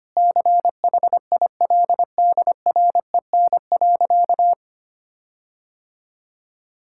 Morse code, heard as a keyed tone, about 25 wpm, 700 Hz. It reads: CHILDREN.
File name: practice / 25wpm / Difficult Sentences with Hints - 25wpm